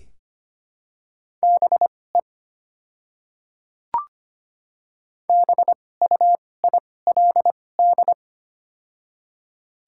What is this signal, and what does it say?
Morse code, heard as a keyed tone, about 25 wpm, spaced as for 18 wpm, 700 Hz. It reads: BE BUILD
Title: 2x Character Spacing - Top 200 Words - 25wpm